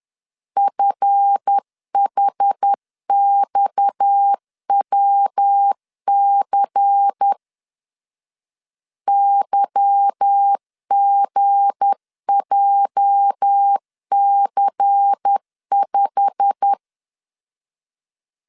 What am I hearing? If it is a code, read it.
Morse code, heard as a keyed tone, about 10 wpm, 800 Hz. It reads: FHXWC YGJC5
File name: LEZIONI / 10 SIGNS / 10A SLASH 10wpm 2808